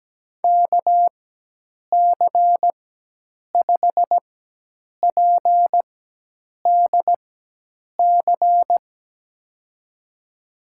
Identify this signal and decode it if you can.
Morse code, heard as a keyed tone, about 17 wpm, spaced as for 8 wpm, 700 Hz. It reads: KC5PDC
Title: USA Call Signs - 6 Characters - 4x Character Spacing - 17wpm